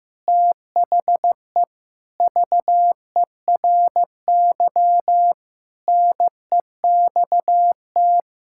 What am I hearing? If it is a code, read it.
Morse code, heard as a keyed tone, about 15 wpm, 700 Hz. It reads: THE VERY NEXT
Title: Sentences from Top 1000 Words - Review Part 6 - 15wpm